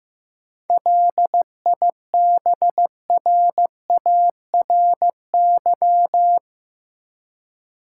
Morse code, heard as a keyed tone, about 15 wpm, 700 Hz. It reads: LIBRARY
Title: Seven Letter Words - 15wpm